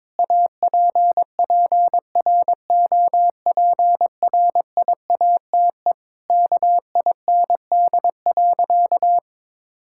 Morse code, heard as a keyed tone, about 22 wpm, 700 Hz. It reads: APPROPRIATE KIND.
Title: Sentences from Top 1000 Words - Review Part 5 - 22wpm